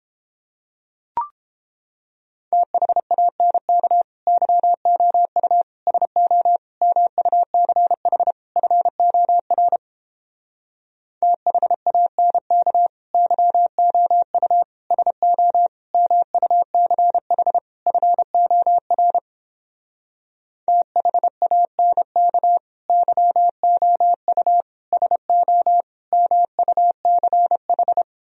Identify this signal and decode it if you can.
Morse code, heard as a keyed tone, about 31 wpm, 700 Hz. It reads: THANK YOU SO MUCH FOR THANK YOU SO MUCH FOR THANK YOU SO MUCH